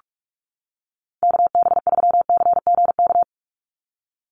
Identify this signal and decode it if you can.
Morse code, heard as a keyed tone, about 44 wpm, 700 Hz. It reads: KB3CGK
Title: USA Call Signs 45wpm